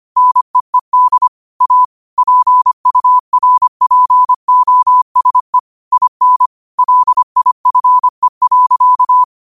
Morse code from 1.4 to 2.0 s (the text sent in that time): A